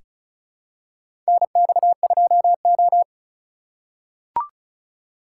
Morse code, heard as a keyed tone, about 34 wpm, 700 Hz. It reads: NX2O
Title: USA Call Signs - 2x Character Spacing - 35wpm